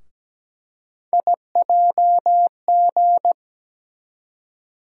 Morse code, heard as a keyed tone, about 17 wpm, 700 Hz. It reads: IJG
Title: USA Call Sign Postfixes - 3 Letters 17wpm